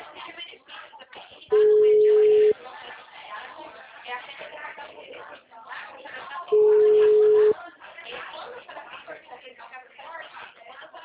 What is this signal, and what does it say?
Ringback tone (Europe, 425 Hz), with speech.